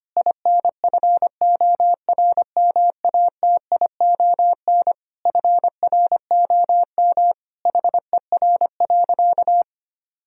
Morse code, heard as a keyed tone, about 25 wpm, 700 Hz. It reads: INFORMATION FROM HER.
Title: Sentences from Top 1000 Words - Review Part 3 - 25wpm